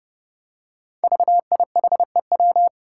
Morse code, heard as a keyed tone, about 30 wpm, 700 Hz. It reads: VIHEW
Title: Code Groups - 5 Letter:Numbers - 30wpm